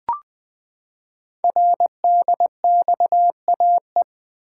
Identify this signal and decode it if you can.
Morse code, heard as a keyed tone, about 20 wpm, 700 Hz. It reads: RDXAE
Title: Code Groups - 5 Letters - 20wpm